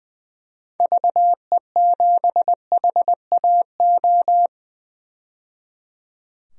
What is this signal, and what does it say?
Morse code, heard as a keyed tone, about 21 wpm, 700 Hz. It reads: VE7HAO
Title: DXCC-100 Call Signs - 6 Characters - 20wpm